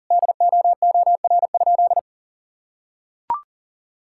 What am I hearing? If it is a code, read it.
Morse code, heard as a keyed tone, about 40 wpm, 700 Hz. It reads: DOOR?
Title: Sentences from Top 1600 Words 40wpm